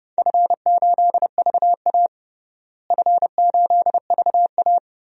Morse code, heard as a keyed tone, about 30 wpm, 700 Hz. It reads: F8VA F8VA
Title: DXCC-100 Call Signs - 4 Characters - Repeated 2x - 30wpm